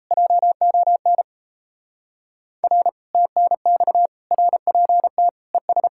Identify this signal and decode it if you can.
Morse code, heard as a keyed tone, about 34 wpm, 700 Hz. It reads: JON R TNX RPT ES
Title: Speed-Racing Common QSO Phrases - 28wpm